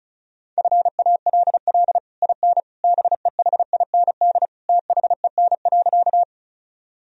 Morse code, heard as a keyed tone, about 36 wpm, 700 Hz. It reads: FALL IN BEHIND THEN.
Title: Sentences from Top 1000 Words - Review Part 2 - 35wpm